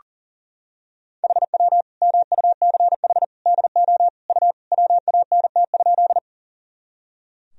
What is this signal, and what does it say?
Morse code, heard as a keyed tone, about 40 wpm, 700 Hz. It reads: HW MUCH DO U WANT?
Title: Sentences from Top 200 Words and Abbreviations 40wpm